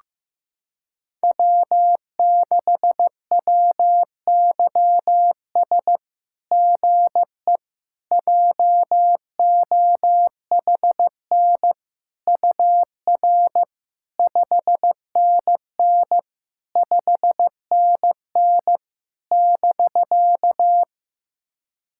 Morse code, heard as a keyed tone, about 16 wpm, 700 Hz. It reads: W6WYS GE JOHN UR 5NN 5NN <BK>
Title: SOTA QSO Practice 15wpm